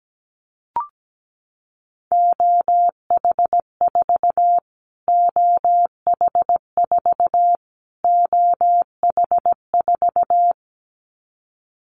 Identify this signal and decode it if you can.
Morse code, heard as a keyed tone, about 17 wpm, 700 Hz. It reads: OH4 OH4 OH4
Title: Callsign Prefixes for Top 25 DXCC - Repeated 3x - 17wpm